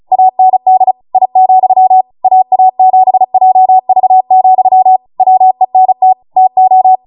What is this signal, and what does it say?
Morse code, heard as a keyed tone, about 36 wpm, 750 Hz. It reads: AND I, AA7JV, WENT TO